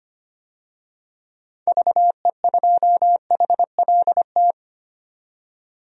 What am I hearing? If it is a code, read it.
Morse code, heard as a keyed tone, about 25 wpm, 700 Hz. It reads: VE2HLT
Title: DXCC-100 Call Signs - 6 Characters - 25wpm